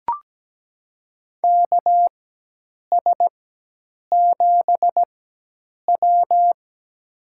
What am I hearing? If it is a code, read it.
Morse code, heard as a keyed tone, about 17 wpm, spaced as for 8 wpm, 700 Hz. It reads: KS7W
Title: USA Call Signs - 4x Character Spacing - 17wpm